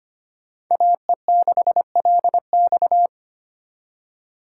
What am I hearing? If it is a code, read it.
Morse code, heard as a keyed tone, about 29 wpm, 700 Hz. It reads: AE6LX
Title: Speed Racing - USA 2x2 Call Signs 25wpm